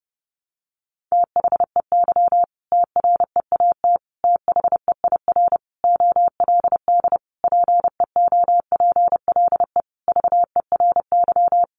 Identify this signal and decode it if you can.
Morse code, heard as a keyed tone, about 30 wpm, 700 Hz. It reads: THEY TREAT THEIR OLD PEOPLE VERY
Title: Sentences from Top 1300 Words 30wpm